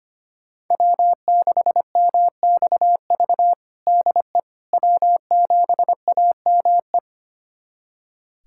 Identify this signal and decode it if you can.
Morse code, heard as a keyed tone, about 25 wpm, 700 Hz. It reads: W6MXV DE W7AME
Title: USA Callsign DE Callsign - 5 Characters - 25wpm